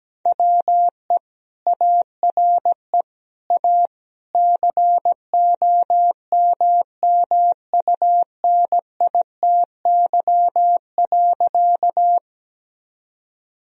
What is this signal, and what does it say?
Morse code, heard as a keyed tone, about 17 wpm, 700 Hz. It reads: WE ARE A COMMUNITY.
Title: Sentences from Top 1000 Words - Review Part 4 - 17wpm